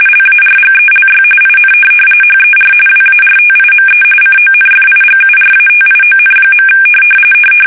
RTTY, about 45.45 baud. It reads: LARGELY IN HER WARTIME CONFIGURATION.
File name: fsk850 / USS-Massachusetts